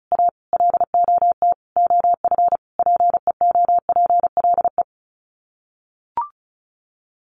Morse code, heard as a keyed tone, about 34 wpm, 700 Hz. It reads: A LOT OF PEOPLE
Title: Sets of 5 Words 35wpm